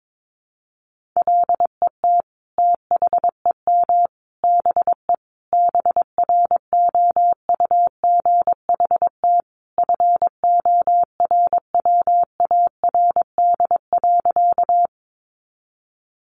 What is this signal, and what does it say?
Morse code, heard as a keyed tone, about 22 wpm, 700 Hz. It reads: LET THEM BE BROUGHT FORWARD.
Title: Sentences from Top 1300 Words 22wpm